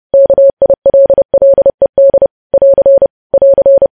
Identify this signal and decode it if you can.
Morse code, heard as a keyed tone, about 30 wpm, 550 Hz. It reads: KILLED <AR> <AR>
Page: news headlines in morse code at 30 wpm